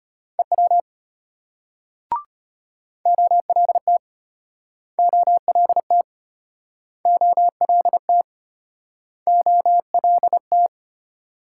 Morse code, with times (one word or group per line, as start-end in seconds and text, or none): 0.4-0.8 EW
3.1-4.0 OLT
5.0-6.0 OLT
7.1-8.2 OLT
9.3-10.7 OLT